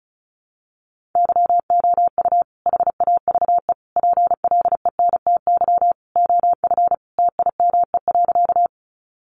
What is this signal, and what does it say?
Morse code, heard as a keyed tone, about 34 wpm, 700 Hz. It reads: YOU HAVE PLENTY OF TIME.
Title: Sentences from Top 1300 Words 35wpm